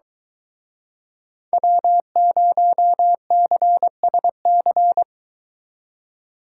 Morse code, heard as a keyed tone, about 24 wpm, 700 Hz. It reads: W0CSC